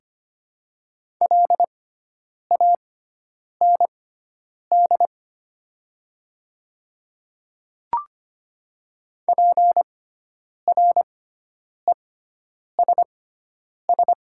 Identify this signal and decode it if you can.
Morse code, heard as a keyed tone, about 25 wpm, spaced as for 9 wpm, 700 Hz. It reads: LAND PRESS
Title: 6x Character Spacing - Top 300 Words - 25wpm